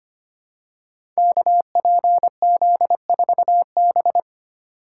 Morse code, heard as a keyed tone, about 25 wpm, 700 Hz. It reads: KPZ4B
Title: Code Groups - 5 Letter:Numbers - 25wpm